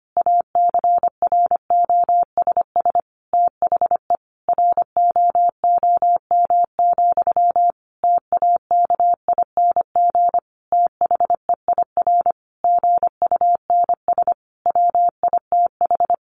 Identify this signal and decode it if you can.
Morse code, heard as a keyed tone, about 25 wpm, 700 Hz. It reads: ACROSS THE ROOM, TAKING THEIR GUNS WITH
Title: A Sound Of Thunder 25wpm